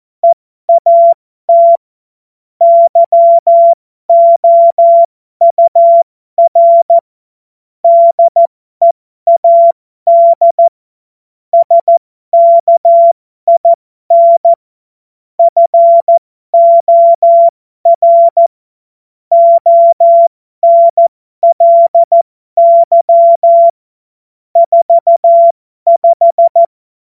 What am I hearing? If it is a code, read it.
Morse code, heard as a keyed tone, about 14 wpm, spaced as for 12 wpm, 680 Hz. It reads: EAT YOUR DEAD SKIN FOR ONLY 45
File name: Morsekurs / Witze